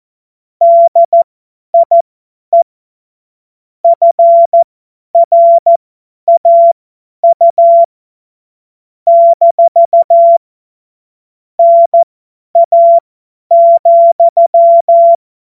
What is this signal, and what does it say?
Morse code, heard as a keyed tone, about 14 wpm, spaced as for 10 wpm, 680 Hz. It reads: DIE FRAU - NA,
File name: Morsekurs / Witze / Witz 09. Februar 2021, 50 F70